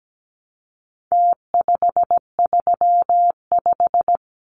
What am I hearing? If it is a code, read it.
Morse code, heard as a keyed tone, about 17 wpm, 700 Hz. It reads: T535
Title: Numbers - Contest Style - 4-Digit 17wpm